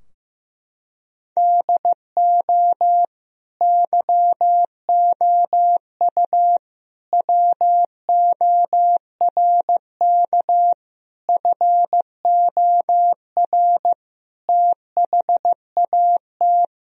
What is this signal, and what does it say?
Morse code, heard as a keyed tone, about 15 wpm, 700 Hz. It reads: DO YOU WORK FOR THAT
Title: Sentences from Top 1000 Words - Review Part 3 - 15wpm